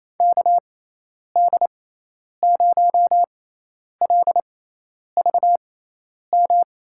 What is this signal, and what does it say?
Morse code, heard as a keyed tone, about 28 wpm, spaced as for 10 wpm, 700 Hz. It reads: KD0LVM
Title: USA Call Signs - 6 Characters - 6x Character Spacing - 28wpm